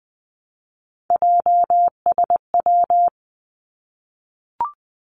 Morse code, heard as a keyed tone, about 20 wpm, 700 Hz. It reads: JSW
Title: Three Challenging Letter-Number Combinations 20wpm